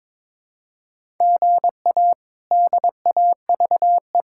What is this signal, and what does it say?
Morse code, heard as a keyed tone, about 21 wpm, 700 Hz. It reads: GA DAVE
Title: K1USN SST QSO Practice - 22wpm